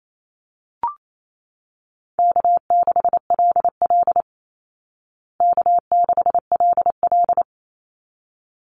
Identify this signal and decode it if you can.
Morse code, heard as a keyed tone, about 28 wpm, 700 Hz. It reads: K6LL K6LL
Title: Easy Call Signs Repeated 2x - 28wpm